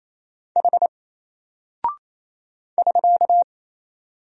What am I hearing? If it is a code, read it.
Morse code, heard as a keyed tone, about 28 wpm, 700 Hz. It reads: H <SK>
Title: Single Character (Letter:Number:Punctuation:Prosign) Alt 28wpm